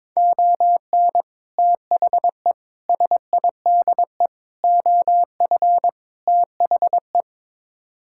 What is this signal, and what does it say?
Morse code, heard as a keyed tone, about 22 wpm, 700 Hz. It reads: ON THE SIDE OF THE
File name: practice / 22wpm / Build A Phrase - 5-Word Phrases - 22wpm